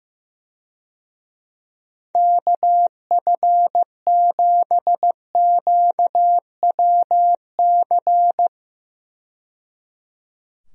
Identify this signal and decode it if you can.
Morse code, heard as a keyed tone, about 15 wpm, 700 Hz. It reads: KF7QWC
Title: USA Call Signs - 6 Characters - 15wpm